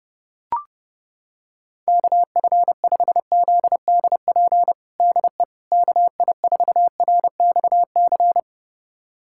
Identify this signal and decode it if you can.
Morse code, heard as a keyed tone, about 30 wpm, 700 Hz. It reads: KF5ZDP DE KI4RXC
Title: USA Callsign DE Callsign 30wpm